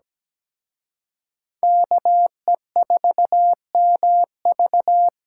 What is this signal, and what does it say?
Morse code, heard as a keyed tone, about 20 wpm, 700 Hz. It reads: KE4MV
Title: Speed Racing - USA 2x2 Call Signs 17wpm